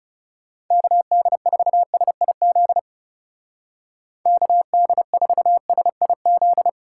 Morse code, heard as a keyed tone, about 36 wpm, 700 Hz. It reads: KD4SIZ KD4SIZ
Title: Speed Racing - USA 2x3 Call Signs 30wpm